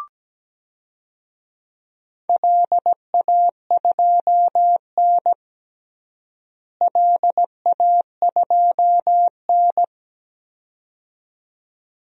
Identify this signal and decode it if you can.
Morse code, heard as a keyed tone, about 17 wpm, 700 Hz. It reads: LA2N LA2N
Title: DXCC-100 Call Signs - 4 Characters - Repeated 2x - 17wpm